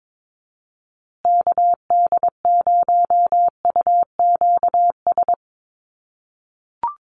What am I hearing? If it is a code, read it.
Morse code, heard as a keyed tone, about 22 wpm, 700 Hz. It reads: KD0UQS
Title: USA Call Signs Repeated 3x 22wpm